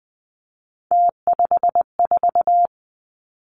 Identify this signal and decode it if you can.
Morse code, heard as a keyed tone, about 20 wpm, 700 Hz. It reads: T54
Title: Three Challenging Letter-Number Combinations 20wpm